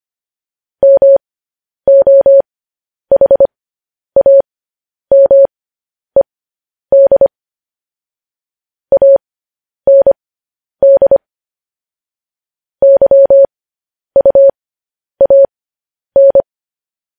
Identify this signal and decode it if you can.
Morse code, heard as a keyed tone, about 25 wpm, spaced as for 10 wpm, 550 Hz. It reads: MOHAMED AND YUAN
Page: news headlines in morse code at 10 wpm